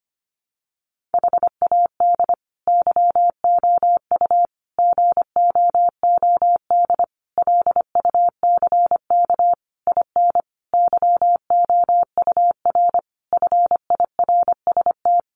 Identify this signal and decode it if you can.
Morse code, heard as a keyed tone, about 25 wpm, 700 Hz. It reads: HAD YOU GOOD LUCK IN YOUR FIRST
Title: Sentences from Top 1500 Words 25wpm